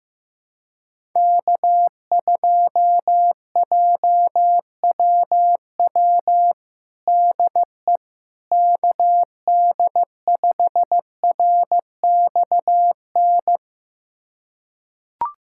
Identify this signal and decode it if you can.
Morse code, heard as a keyed tone, about 15 wpm, 700 Hz. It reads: K2JWW DE KD5RXN
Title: USA Callsign DE Callsign 15wpm